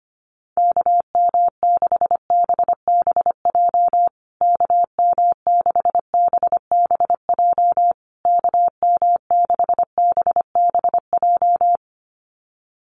Morse code, heard as a keyed tone, about 25 wpm, 700 Hz. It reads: KM6BBJ KM6BBJ KM6BBJ
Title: USA Call Signs Repeated 3x 25wpm